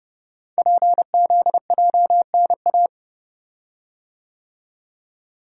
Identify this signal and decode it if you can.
Morse code, heard as a keyed tone, about 30 wpm, 700 Hz. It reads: PZJNA
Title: Code Groups - 5 Letters - 30wpm